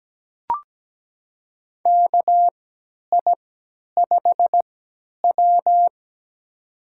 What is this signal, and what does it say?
Morse code, heard as a keyed tone, about 17 wpm, spaced as for 10 wpm, 700 Hz. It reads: KI5W
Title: USA Call Signs - 4 Characters - 3x Character Spacing - 17wpm